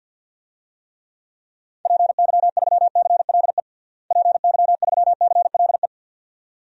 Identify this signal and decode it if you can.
Morse code, heard as a keyed tone, about 53 wpm, 700 Hz. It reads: PY3CLE PY3CLE
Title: DXCC-100 Call Signs - 6 Characters - Repeated 2x - 50wpm